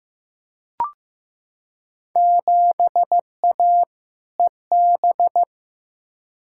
Morse code, heard as a keyed tone, about 15 wpm, 700 Hz. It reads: 7A EB
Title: Field Day Exchanges Encore - 15wpm